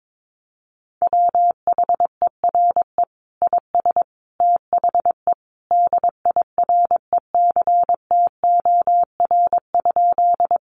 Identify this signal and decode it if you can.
Morse code, heard as a keyed tone, about 22 wpm, 700 Hz. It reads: WHERE IS THE DIRECTOR?
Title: Sentences from Top 1300 Words 22wpm